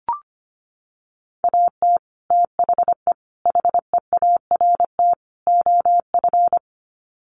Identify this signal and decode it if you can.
Morse code, heard as a keyed tone, about 25 wpm, 700 Hz. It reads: AT THE HEART OF
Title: Sets of 4 Words 25wpm